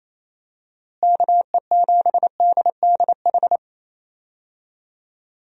Morse code, heard as a keyed tone, about 27 wpm, 700 Hz. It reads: KE7DDH
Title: USA Call Signs - 6 Characters - Repeated 3x - 28wpm